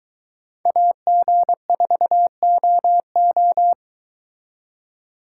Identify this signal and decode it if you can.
Morse code, heard as a keyed tone, about 22 wpm, 700 Hz. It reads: AG4OO